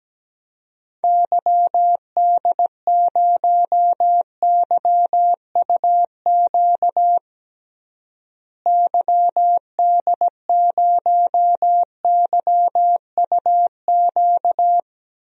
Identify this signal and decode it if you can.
Morse code, heard as a keyed tone, about 17 wpm, 700 Hz. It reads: YD0YUQ YD0YUQ
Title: DXCC-100 Call Signs - 6 Characters - Repeated 2x - 17wpm